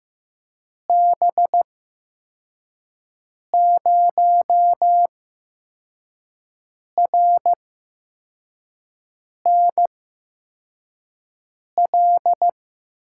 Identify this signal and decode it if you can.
Morse code, heard as a keyed tone, about 15 wpm, spaced as for 4 wpm, 700 Hz. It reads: B0RNL
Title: USA Call Signs - 6 Characters - 8x Character Spacing - 15wpm